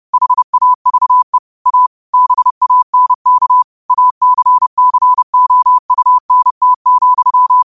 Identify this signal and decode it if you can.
Morse code, heard as a keyed tone, about 30 wpm, 1000 Hz. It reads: HAVE A BANK ACCOUNT,